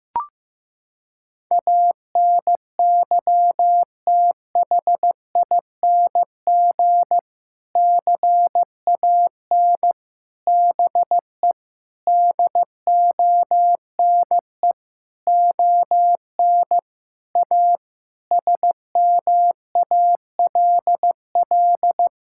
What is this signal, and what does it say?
Morse code, heard as a keyed tone, about 15 wpm, 700 Hz. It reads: ANYTHING CAN BE DONE ON A SMALL
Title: Sentences from Top 1000 Words - Review Part 5 - 15wpm